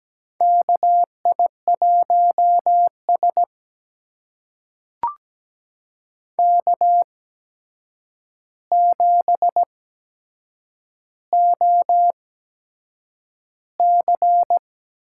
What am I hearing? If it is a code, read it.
Morse code, heard as a keyed tone, about 17 wpm, spaced as for 7 wpm, 700 Hz. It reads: KI1S K7OC
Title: USA Call Signs - 4 Characters - 8x Character Spacing - 17wpm